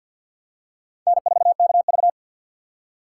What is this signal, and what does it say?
Morse code, heard as a keyed tone, about 47 wpm, 700 Hz. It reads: N4KV
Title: USA Call Signs - 4 Characters - 6x Character Spacing - 50wpm